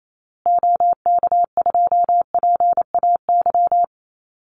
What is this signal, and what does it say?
Morse code, heard as a keyed tone, about 28 wpm, 700 Hz. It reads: OK2PAY
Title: Call Signs Repeated 3x - 28wpm